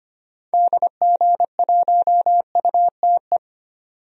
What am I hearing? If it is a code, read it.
Morse code, heard as a keyed tone, about 25 wpm, 700 Hz. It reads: DG1UTE
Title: DXCC-100 Call Signs - 6 Characters - 25wpm